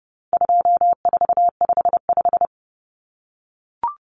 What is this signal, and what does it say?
Morse code, heard as a keyed tone, about 28 wpm, 700 Hz. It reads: 2455
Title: Common Cut Numbers - 4-Digit 30wpm